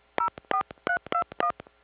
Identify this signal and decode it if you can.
DTMF, heard as telephone touch tones, tones of about 90 ms. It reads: 04321